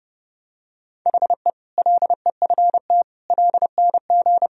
Morse code, heard as a keyed tone, about 30 wpm, 700 Hz. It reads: HE LEFT LNG
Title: Sentences from Top 500 Words and Abbreviations 30wpm